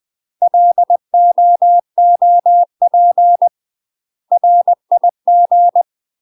Morse code, heard as a keyed tone, about 20 wpm, 700 Hz. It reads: LOOP RIG
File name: UK MASTER QSOs TRAINING / 20WPM / UK MASTER QSOs 0039